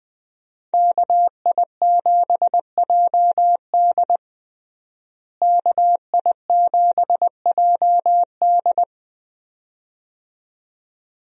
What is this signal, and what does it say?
Morse code, heard as a keyed tone, about 20 wpm, 700 Hz. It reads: KI7JD KI7JD
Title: USA Call Signs - 5 Characters - Repeated 2x - 20wpm